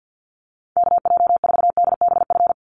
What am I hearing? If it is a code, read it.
Morse code, heard as a keyed tone, about 50 wpm, 700 Hz. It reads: KJ4DBF
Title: USA Call Signs 50wpm